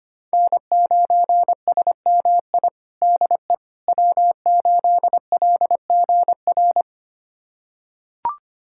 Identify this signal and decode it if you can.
Morse code, heard as a keyed tone, about 25 wpm, 700 Hz. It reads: N9SMI DE W8LGR
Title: USA Callsign DE Callsign - 5 Characters - 25wpm